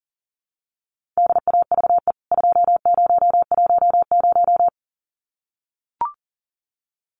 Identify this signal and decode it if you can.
Morse code, heard as a keyed tone, about 40 wpm, 700 Hz. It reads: DAVE 2010
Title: CWT Exchange 40wpm